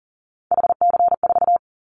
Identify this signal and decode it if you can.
Morse code, heard as a keyed tone, about 40 wpm, 700 Hz. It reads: HC4
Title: Callsign Prefixes for Top 100 DXCC - 40wpm